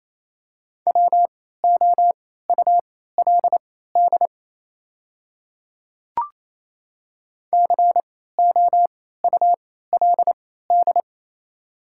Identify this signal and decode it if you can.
Morse code, heard as a keyed tone, about 28 wpm, spaced as for 16 wpm, 700 Hz. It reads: WOULD COULD
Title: 3x Character Spacing - Top 300 Words - 28wpm